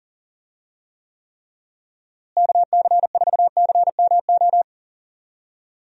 Morse code, heard as a keyed tone, about 40 wpm, 700 Hz. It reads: KC4CMO
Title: USA Call Signs - 6 Characters - Repeated 3x - 40wpm